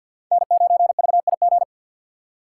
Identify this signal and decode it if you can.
Morse code, heard as a keyed tone, about 49 wpm, 700 Hz. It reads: N9VIG
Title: USA Call Signs - 5 Characters - 2x Character Spacing - 50wpm